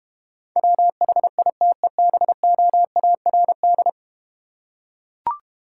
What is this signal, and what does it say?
Morse code, heard as a keyed tone, about 32 wpm, 700 Hz. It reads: WHITEBOARD